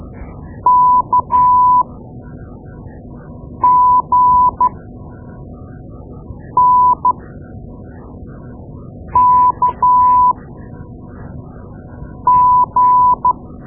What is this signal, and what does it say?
Morse code, heard as a keyed tone, about 12 wpm, spaced as for 4 wpm, 1000 Hz. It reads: KGNKG
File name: Morse Lessons / Morse Lessons / Tape-2A